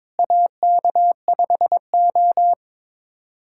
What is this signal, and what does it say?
Morse code, heard as a keyed tone, about 22 wpm, 700 Hz. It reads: AK5O
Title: USA Callsign DE Callsign - 4 Characters - 22wpm